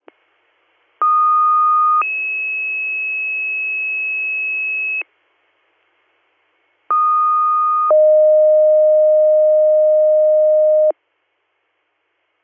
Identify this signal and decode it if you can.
Two-tone page, heard as a one-second tone, then a three-second tone, about 1230 then 2600 Hz, about 1230 then 620 Hz. It is heard from 1.0 to 10.9 s.